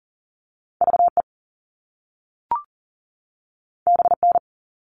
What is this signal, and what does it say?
Morse code, heard as a keyed tone, about 40 wpm, 700 Hz. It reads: VE BN
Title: All Cut Numbers - 2-Digit 40wpm